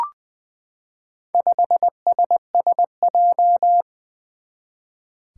Morse code, heard as a keyed tone, about 21 wpm, 700 Hz. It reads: 5SSJ
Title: Four Challenging Letter-Number Combinations 20wpm